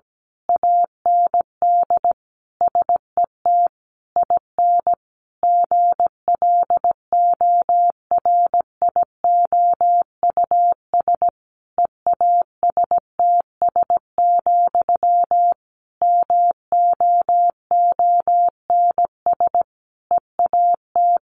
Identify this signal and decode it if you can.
Morse code, heard as a keyed tone, about 17 wpm, 700 Hz. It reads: AND SET IN GLORIOUS EASTS, MOONS EAT
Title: A Sound Of Thunder 17wpm